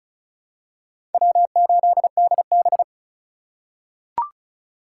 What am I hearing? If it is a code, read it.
Morse code, heard as a keyed tone, about 36 wpm, 700 Hz. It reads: W8DB
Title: USA Call Signs - 4 Characters - 2x Character Spacing - 35wpm